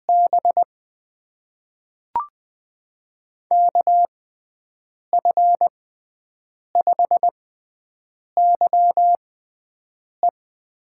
Morse code, heard as a keyed tone, about 20 wpm, spaced as for 7 wpm, 700 Hz. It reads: B KF5YE
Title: USA Call Signs - 6x Character Spacing - 20wpm